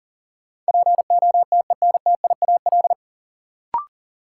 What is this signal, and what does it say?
Morse code, heard as a keyed tone, about 40 wpm, 700 Hz. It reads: POTENTIAL
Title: Sentences from Top 1000 Words - Review Part 4 - New Word Drill - 40wpm